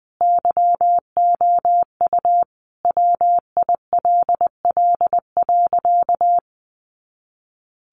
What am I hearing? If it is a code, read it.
Morse code, heard as a keyed tone, about 20 wpm, 700 Hz. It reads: YOU WILL.
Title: Sentences from Top 200 Words 20wpm